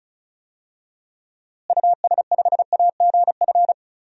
Morse code, heard as a keyed tone, about 35 wpm, 700 Hz. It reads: US5AGF
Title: DXCC-100 Call Signs - 6 Characters - Repeated 2x - 35wpm